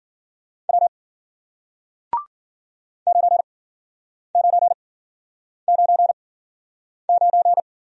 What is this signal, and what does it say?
Morse code, heard as a keyed tone, about 50 wpm, spaced as for 9 wpm, 700 Hz. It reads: W 9999